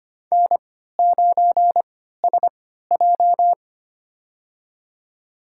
Morse code, heard as a keyed tone, about 25 wpm, spaced as for 14 wpm, 700 Hz. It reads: N9SJ
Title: USA Call Signs - 4 Characters - 3x Character Spacing - 25wpm